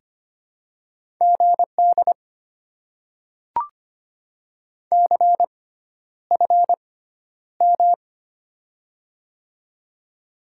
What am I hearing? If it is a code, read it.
Morse code, heard as a keyed tone, about 25 wpm, spaced as for 11 wpm, 700 Hz. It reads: GD CFM
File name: Top 90 QSO Elements - 6x Character Spacing - 25wpm